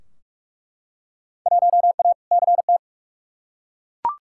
Morse code, heard as a keyed tone, about 47 wpm, 700 Hz. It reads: 1A CT
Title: Field Day Exchanges - 45wpm